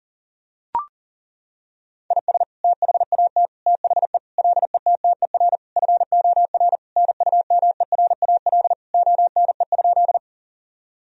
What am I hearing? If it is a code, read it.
Morse code, heard as a keyed tone, about 37 wpm, 700 Hz. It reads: IS THAT THE LETTER FOR NUMERAL ONE?
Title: Sentences from Top 1000 Words - Review Part 2 - 40wpm